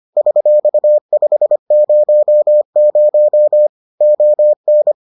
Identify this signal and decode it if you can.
Morse code, heard as a keyed tone, about 25 wpm, 600 Hz. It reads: $500 ON